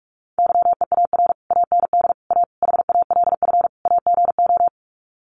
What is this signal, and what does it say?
Morse code, heard as a keyed tone, about 44 wpm, 700 Hz. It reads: YEAR AND A HALF AGO
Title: Sets of 5 Words Encore 45wpm